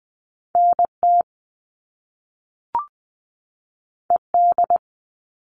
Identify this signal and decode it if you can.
Morse code, heard as a keyed tone, about 20 wpm, 700 Hz. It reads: NT ED
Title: All Cut Numbers - 2-Digit 20wpm